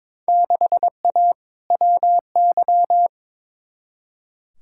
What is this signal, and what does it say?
Morse code, heard as a keyed tone, about 22 wpm, 700 Hz. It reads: 6A WY
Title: Field Day Exchanges Encore - 22wpm